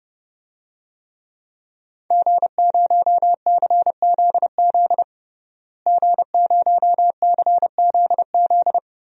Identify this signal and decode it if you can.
Morse code, heard as a keyed tone, about 30 wpm, 700 Hz. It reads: G0CZZ G0CZZ
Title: DXCC-100 Call Signs - 5 Characters - Repeated 3x - 30wpm